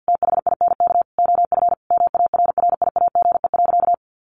Morse code, heard as a keyed tone, about 50 wpm, 700 Hz. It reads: THINK OF MARRIAGE.
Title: Sentences from Top 800 Words 50wpm